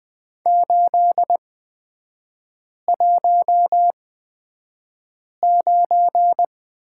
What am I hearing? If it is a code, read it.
Morse code, heard as a keyed tone, about 20 wpm, spaced as for 5 wpm, 700 Hz. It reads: 819
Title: 236 Lesson 38e - Letter q - Three Characters 20wpm